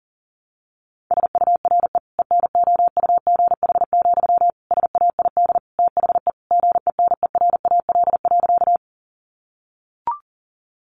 Morse code, heard as a keyed tone, about 40 wpm, 700 Hz. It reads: SURE ENOUGH, SAID THE GENERAL.
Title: Sentences from Top 600 Words 40wpm